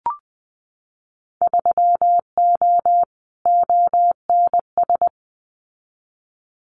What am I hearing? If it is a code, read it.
Morse code, heard as a keyed tone, about 20 wpm, 700 Hz. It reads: 3O ONS
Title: Winter Field Day Exchanges 20wpm